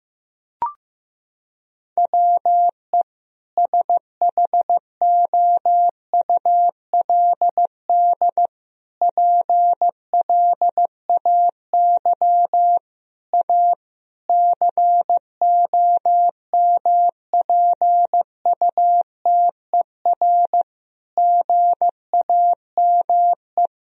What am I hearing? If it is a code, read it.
Morse code, heard as a keyed tone, about 15 wpm, 700 Hz. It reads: WE SHOULD PLAY A COMPUTER GAME
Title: Sentences from Top 1000 Words - Review Part 3 - 15wpm